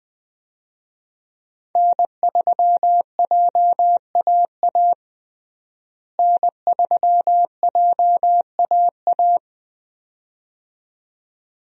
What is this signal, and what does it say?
Morse code, heard as a keyed tone, about 20 wpm, 700 Hz. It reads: N3JAA N3JAA
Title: USA Call Signs - 5 Characters - Repeated 2x - 20wpm